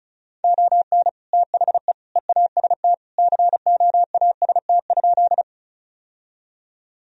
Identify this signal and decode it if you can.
Morse code, heard as a keyed tone, about 34 wpm, 700 Hz. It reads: ON THE EAST COAST?
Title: Sentences from Top 1000 Words - Review Part 4 - 35wpm